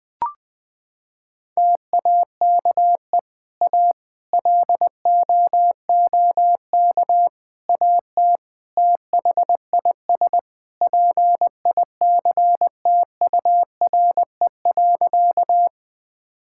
Morse code, heard as a keyed tone, about 21 wpm, 700 Hz. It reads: TAKE A LOOK AT THIS PICTURE.
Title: Sentences with 4-Word Phrases - 20wpm